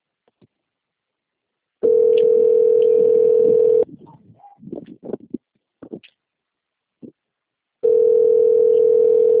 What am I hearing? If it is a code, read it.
Ringback tone (North America), 440 + 480 Hz.
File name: rg-1600-unknown- - - .21721